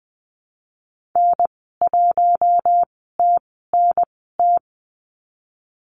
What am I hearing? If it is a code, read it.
Morse code, heard as a keyed tone, about 20 wpm, spaced as for 15 wpm, 700 Hz. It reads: N1TNT
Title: 069 Lesson 12g - Letter d - Call Signs 20wpm